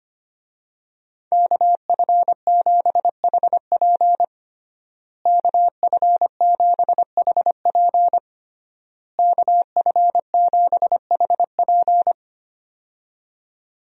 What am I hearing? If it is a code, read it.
Morse code, heard as a keyed tone, about 25 wpm, 700 Hz. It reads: KF7HP KF7HP KF7HP
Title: USA Call Signs - 5 Characters - Repeated 3x - 25wpm